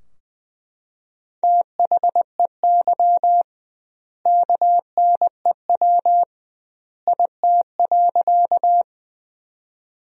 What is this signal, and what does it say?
Morse code, heard as a keyed tone, about 21 wpm, 700 Hz. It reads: THEY KNEW IT.
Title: 2x Word-Spacing - Sentences from Top 400 Words - 20wpm